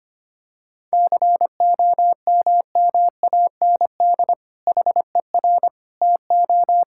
Morse code, heard as a keyed tone, about 25 wpm, 700 Hz. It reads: COMMAND HER TO
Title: Sentences from Top 1600 Words 25wpm